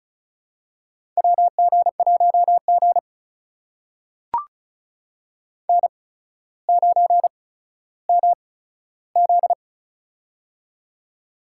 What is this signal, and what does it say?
Morse code, heard as a keyed tone, about 34 wpm, spaced as for 15 wpm, 700 Hz. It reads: WG1G N9MZ
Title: USA Call Signs - 4 Characters - 8x Character Spacing - 35wpm